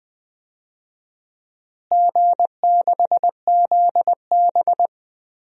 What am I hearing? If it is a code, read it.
Morse code, heard as a keyed tone, about 20 wpm, 700 Hz. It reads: G6ZB
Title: DXCC-100 Call Signs - 4 Characters - Repeated 3x - 20wpm